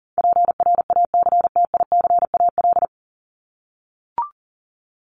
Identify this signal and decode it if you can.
Morse code, heard as a keyed tone, about 40 wpm, 700 Hz. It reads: PRACTICAL
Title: Top 1251-1500 Words 40wpm